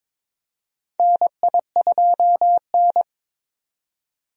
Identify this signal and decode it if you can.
Morse code, heard as a keyed tone, about 21 wpm, 700 Hz. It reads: NI2N
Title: USA Call Signs - 4 Characters - 22wpm